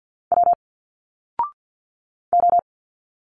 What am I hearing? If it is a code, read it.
Morse code, heard as a keyed tone, about 51 wpm, 700 Hz. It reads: W O
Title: Single Letters 50wpm